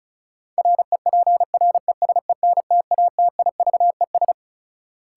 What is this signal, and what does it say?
Morse code, heard as a keyed tone, about 34 wpm, 700 Hz. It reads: REPRESENTATIVES
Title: Sets of 5 Words Encore - New Word Drill - 35wpm